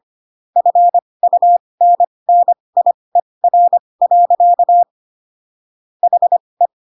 Morse code, heard as a keyed tone, about 25 wpm, spaced as for 20 wpm, 700 Hz. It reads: FUNNIER. HE
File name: Pinocchio / Pinocchio ch1 15 25 ik1hge